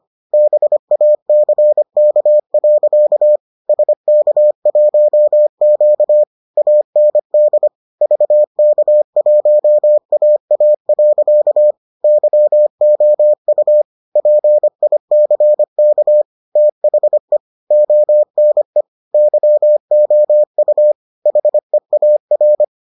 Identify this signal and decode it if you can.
Morse code, heard as a keyed tone, about 25 wpm, 600 Hz. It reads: BACK. SK1Q AND VK1AA. YOU PICK THE ONE YOU HEAR